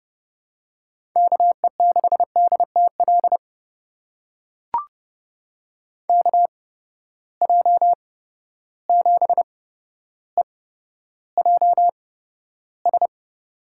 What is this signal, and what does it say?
Morse code, heard as a keyed tone, about 32 wpm, spaced as for 13 wpm, 700 Hz. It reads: KE6DTL KJ7EJS
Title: USA Call Signs - 8x Character Spacing - 30wpm